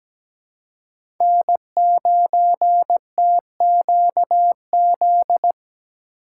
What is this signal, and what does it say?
Morse code, heard as a keyed tone, about 17 wpm, 700 Hz. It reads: N9TQZ
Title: USA Call Signs - 5 Characters - 4x Character Spacing - 17wpm